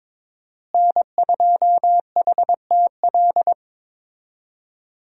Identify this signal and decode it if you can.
Morse code, heard as a keyed tone, about 22 wpm, 700 Hz. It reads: N2HTL
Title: USA Call Signs - 5 Characters - Repeated 3x - 22wpm